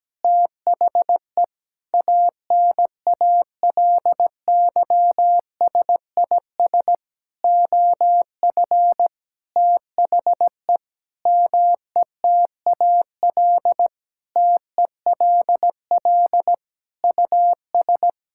Morse code, heard as a keyed tone, about 17 wpm, 700 Hz. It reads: THE ANALYSIS OF THE METAL TELL US
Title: Sentences from Top 1000 Words - Review Part 6 - 17wpm